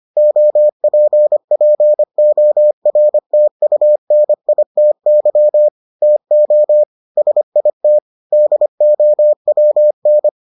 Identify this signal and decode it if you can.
Morse code, heard as a keyed tone, about 25 wpm, 600 Hz. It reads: OPPORTUNITY TO SIT DOWN